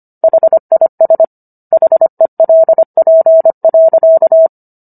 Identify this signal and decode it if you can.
Morse code, heard as a keyed tone, about 25 wpm, 650 Hz. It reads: HIS HELP.